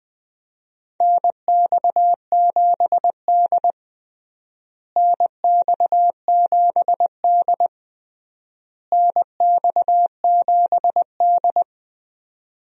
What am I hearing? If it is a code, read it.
Morse code, heard as a keyed tone, about 20 wpm, 700 Hz. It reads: NX7D NX7D NX7D
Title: USA Call Signs - 4 Characters - Repeated 3x - 20wpm